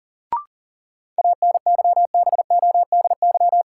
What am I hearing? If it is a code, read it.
Morse code, heard as a keyed tone, about 40 wpm, 700 Hz. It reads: ANYBODY
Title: Sets of 5 Words - New Word Drill - 40wpm